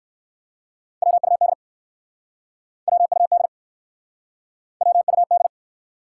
Morse code, heard as a keyed tone, about 59 wpm, 700 Hz. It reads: WUD WUD WUD